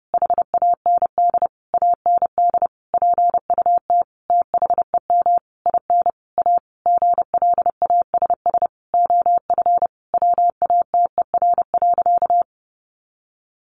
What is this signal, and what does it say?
Morse code, heard as a keyed tone, about 30 wpm, 700 Hz. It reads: HAND AND PUT THEM IN A GLASS OF WATER.
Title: Sentences from Top 1300 Words 30wpm